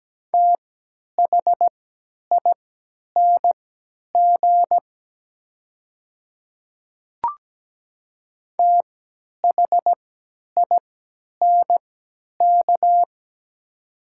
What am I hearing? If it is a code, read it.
Morse code, heard as a keyed tone, about 17 wpm, spaced as for 10 wpm, 700 Hz. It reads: THING THINK
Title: 3x Character Spacing - Top 500 Words - 17wpm